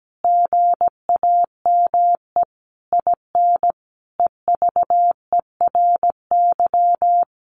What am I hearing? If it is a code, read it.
Morse code, heard as a keyed tone, about 17 wpm, 700 Hz. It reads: GAME IN EVERY
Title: Sentences from Top 1500 Words 17wpm